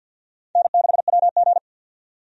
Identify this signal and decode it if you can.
Morse code, heard as a keyed tone, about 48 wpm, 700 Hz. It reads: N6WG
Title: USA Call Signs - 4 Characters - 50wpm